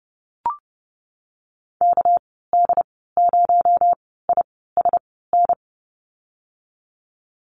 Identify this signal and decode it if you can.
Morse code, heard as a keyed tone, about 30 wpm, spaced as for 17 wpm, 700 Hz. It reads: KD0ISN
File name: Lesson 37g - Letter x - Call Signs 30wpm